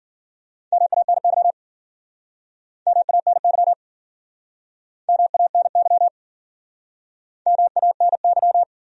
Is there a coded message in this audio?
Yes: Morse code from 0.7 to 8.6 s, heard as a keyed tone, about 54 wpm, 700 Hz. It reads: MANY MANY MANY MANY